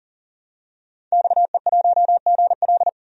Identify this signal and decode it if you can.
Morse code, heard as a keyed tone, about 40 wpm, 700 Hz. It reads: XE1GL
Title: DXCC-100 Call Signs - 5 Characters - 40wpm